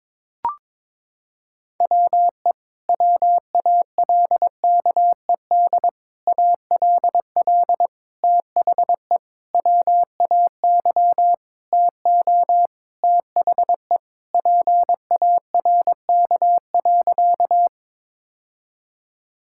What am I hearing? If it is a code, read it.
Morse code, heard as a keyed tone, about 22 wpm, 700 Hz. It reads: WE WALKED ALL THE WAY TO THE PARK.
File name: practice / 22wpm / Sentences with 4-Word Phrases - 22wpm